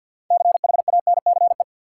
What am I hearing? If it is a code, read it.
Morse code, heard as a keyed tone, about 49 wpm, 700 Hz. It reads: CHANCE
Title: Six Letter Words - 50wpm